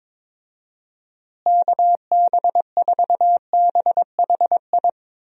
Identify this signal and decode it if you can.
Morse code, heard as a keyed tone, about 21 wpm, 700 Hz. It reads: KB4BHI
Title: USA Call Signs - 6 Characters - Repeated 2x - 22wpm